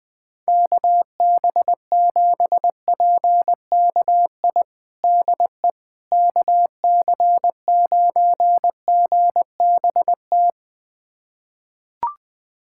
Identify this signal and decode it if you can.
Morse code, heard as a keyed tone, about 20 wpm, 700 Hz. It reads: KB7PKI DE KC9GBT
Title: USA Callsign DE Callsign - 6 Characters - 20wpm